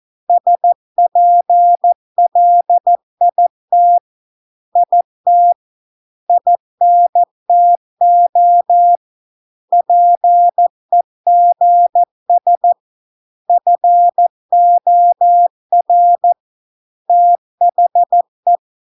Morse code, heard as a keyed tone, about 14 wpm, 700 Hz. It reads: SPLIT IT INTO PEGS FOR THE